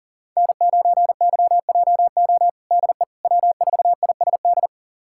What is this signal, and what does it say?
Morse code, heard as a keyed tone, about 40 wpm, 700 Hz. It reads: N9YJO DE W4ISD
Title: USA Callsign DE Callsign - 5 Characters - 40wpm